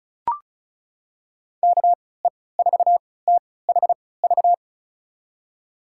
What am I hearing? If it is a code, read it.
Morse code, heard as a keyed tone, about 36 wpm, spaced as for 20 wpm, 700 Hz. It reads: KE4THV
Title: USA Call Signs - 6 Characters - 3x Character Spacing - 35wpm